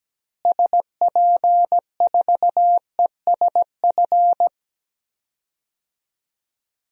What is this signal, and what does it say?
Morse code, heard as a keyed tone, about 17 wpm, 700 Hz. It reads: SP4ESF
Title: DXCC-100 Call Signs - 6 Characters - Repeated 2x - 17wpm